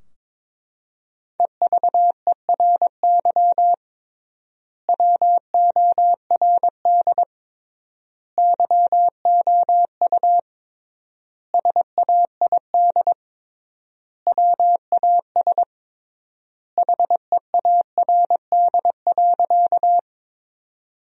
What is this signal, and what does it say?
Morse code, heard as a keyed tone, about 22 wpm, 700 Hz. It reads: EVERY WORD YOU SAID WAS HEARD.
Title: 3x Word-Spacing - Sentences from Top 400 Words - 22wpm